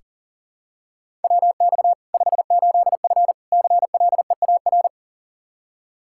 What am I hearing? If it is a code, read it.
Morse code, heard as a keyed tone, about 40 wpm, 700 Hz. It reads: WX 58F CLEAR
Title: Practice QSO - Exchange 2 - Info - 40wpm